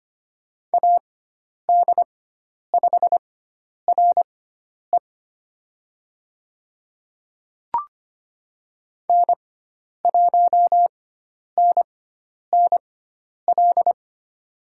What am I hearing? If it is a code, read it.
Morse code, heard as a keyed tone, about 25 wpm, spaced as for 10 wpm, 700 Hz. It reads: AD5RE N1NNL
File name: Lesson 16g - Letter c - Call Signs 25wpm